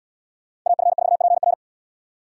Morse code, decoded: AF4PR

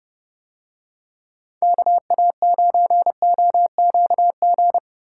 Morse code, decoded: KA9OQG

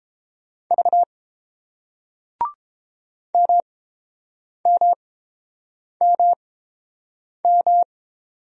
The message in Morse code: V MMMM

